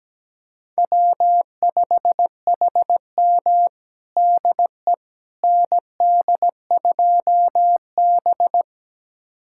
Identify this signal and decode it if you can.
Morse code, heard as a keyed tone, about 17 wpm, 700 Hz. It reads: W5HM DE ND2B
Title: USA Callsign DE Callsign - 4 Characters - 17wpm